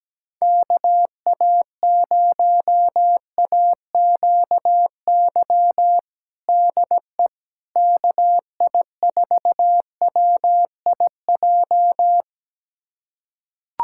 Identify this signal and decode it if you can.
Morse code, heard as a keyed tone, about 17 wpm, 700 Hz. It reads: KA0AQY DE KI4WIJ
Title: USA Callsign DE Callsign - 6 Characters - 17wpm